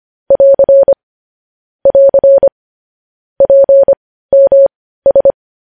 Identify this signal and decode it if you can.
Morse code, heard as a keyed tone, about 25 wpm, spaced as for 15 wpm, 550 Hz. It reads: <AR> <AR> PMS
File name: podcast / News headlines-1x-25x15